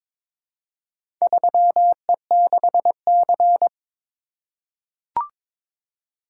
Morse code, heard as a keyed tone, about 22 wpm, 700 Hz. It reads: 3E6C
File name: Four Letter-Number Combinations - 22wpm